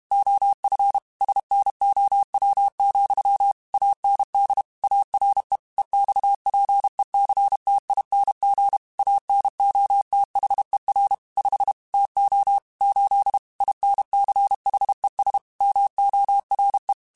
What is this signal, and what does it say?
Morse code, heard as a keyed tone, about 32 wpm, 800 Hz. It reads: OF SNOW, AND ARE EXPECTING ANOTHER 5 TO 8 INCHES MORE
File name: QSOs / QSOs-mp3s / WA7QOX 1@30